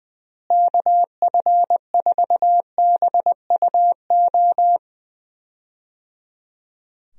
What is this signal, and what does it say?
Morse code, heard as a keyed tone, about 20 wpm, 700 Hz. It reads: KF4BUO
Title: USA Call Signs - 6 Characters - Repeated 3x - 20wpm